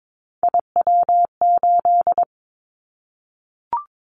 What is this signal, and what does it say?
Morse code, heard as a keyed tone, about 22 wpm, 700 Hz. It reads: IW8
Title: Callsign Prefixes for Top 50 DXCC - 22wpm